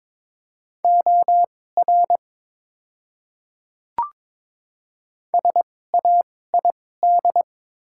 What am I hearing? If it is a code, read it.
Morse code, heard as a keyed tone, about 22 wpm, spaced as for 16 wpm, 690 Hz. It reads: OR SAID
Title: 2x Character Spacing - Top 300 Words - 22wpm